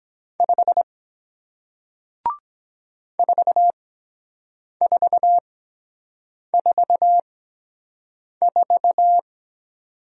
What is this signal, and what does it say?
Morse code, heard as a keyed tone, about 23 wpm, spaced as for 7 wpm, 700 Hz. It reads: 5 4444